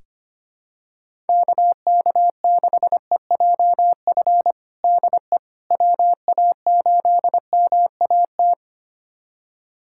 Morse code, heard as a keyed tone, about 25 wpm, 700 Hz. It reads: KK6EJF DE WA8MAT
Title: USA Callsign DE Callsign 25wpm